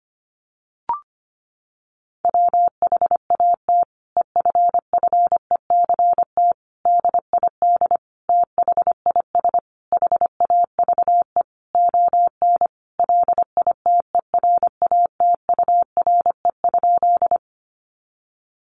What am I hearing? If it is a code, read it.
Morse code, heard as a keyed tone, about 25 wpm, 700 Hz. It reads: WHAT EFFECT DID THIS HAVE ON LITERATURE?
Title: Sentences from Top 1000 Words 25wpm